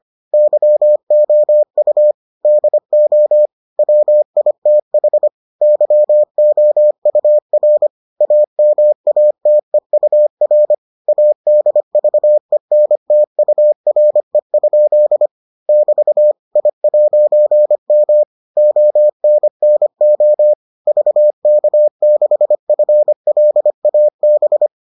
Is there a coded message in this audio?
Yes: Morse code from 0.3 to 24.7 s, heard as a keyed tone, about 25 wpm, 600 Hz. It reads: YOU DO WITH YOUR AMATEUR ADVENTURE? <BT> I'M ONNO VK6FLAB